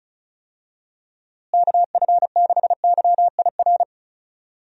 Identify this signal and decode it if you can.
Morse code, heard as a keyed tone, about 36 wpm, 700 Hz. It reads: KF6YIR
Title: USA Call Signs - 6 Characters - Repeated 2x - 35wpm